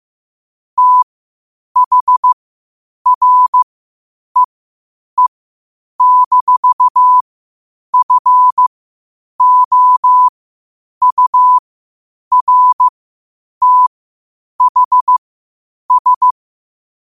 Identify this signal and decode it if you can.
Morse code, heard as a keyed tone, about 15 wpm, spaced as for 8 wpm, 1000 Hz. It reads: THREE-FOURTHS
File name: qotdfiles / 8wpm